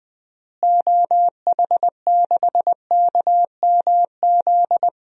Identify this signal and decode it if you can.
Morse code, heard as a keyed tone, about 20 wpm, 700 Hz. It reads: OH6KMZ
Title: DXCC-100 Call Signs - 6 Characters - Repeated 3x - 20wpm